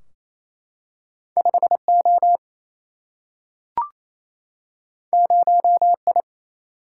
Morse code, heard as a keyed tone, about 28 wpm, 700 Hz. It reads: 5 O 0 I
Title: Single-Letter Single-Number 28wpm